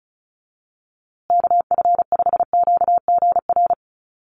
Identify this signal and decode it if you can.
Morse code, heard as a keyed tone, about 34 wpm, 700 Hz. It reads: KF5QGR
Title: USA Call Signs Repeated 3x 35wpm